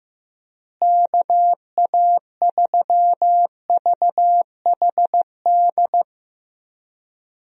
Morse code, heard as a keyed tone, about 15 wpm, 700 Hz. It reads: KA3VHD
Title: USA Call Signs - 6 Characters - Repeated 2x - 15wpm